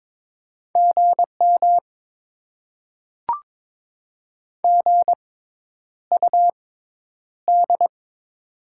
Morse code, heard as a keyed tone, about 23 wpm, spaced as for 10 wpm, 690 Hz. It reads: GM GUD
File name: Top 20 QSO Elements - 6x Character Spacing - 22wpm